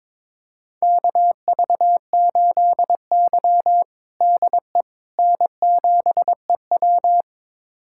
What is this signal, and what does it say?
Morse code, heard as a keyed tone, about 22 wpm, 700 Hz. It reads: KV8Y DE N7EW
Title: USA Callsign DE Callsign - 4 Characters - 22wpm